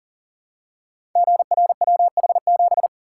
Morse code, heard as a keyed tone, about 39 wpm, 700 Hz. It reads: GRWH7
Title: Code Groups - 5 Letter:Numbers - 40wpm